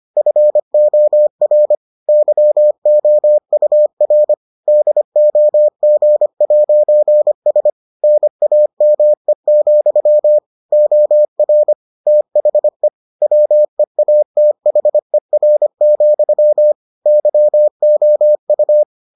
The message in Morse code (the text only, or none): FOR YOUR DOG'S NAME, OR THE WEATHER, YOU